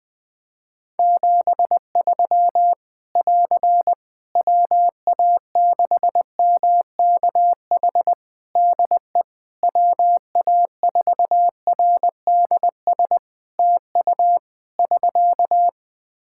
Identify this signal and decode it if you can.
Morse code, heard as a keyed tone, about 20 wpm, 700 Hz. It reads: 73 <AR> WA6MKH DE WA4RDS TU <SK>
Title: Practice QSO - Exchange 4 - Ending - 20wpm